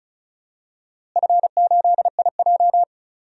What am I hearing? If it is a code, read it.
Morse code, heard as a keyed tone, about 34 wpm, 700 Hz. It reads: F8IJ
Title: DXCC-100 Call Signs - 4 Characters - Repeated 3x - 35wpm